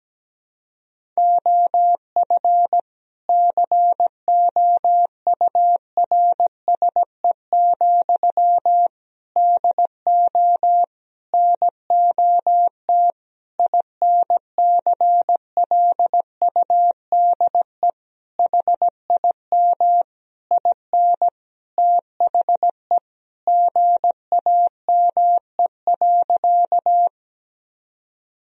Morse code, heard as a keyed tone, about 17 wpm, 700 Hz. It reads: OF COURSE, DO NOT INCLUDE HIM IN THE GAME.
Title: Sentences from Top 1000 Words - Review Part 3 - 17wpm